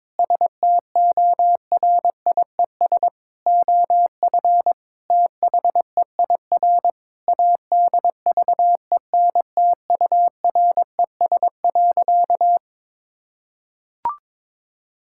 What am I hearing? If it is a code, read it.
Morse code, heard as a keyed tone, about 22 wpm, 700 Hz. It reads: STORIES OF THEIR ADVENTURES.
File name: practice / 22wpm / Sentences with 2-Word Phrases - 22wpm